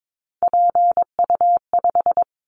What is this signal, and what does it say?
Morse code, heard as a keyed tone, about 22 wpm, 700 Hz. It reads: PU5
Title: Callsign Prefixes for Top 50 DXCC - 22wpm